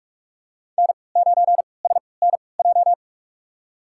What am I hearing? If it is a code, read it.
Morse code, heard as a keyed tone, about 46 wpm, spaced as for 25 wpm, 700 Hz. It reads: N9SNJ
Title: USA Call Signs - 5 Characters - 3x Character Spacing - 45wpm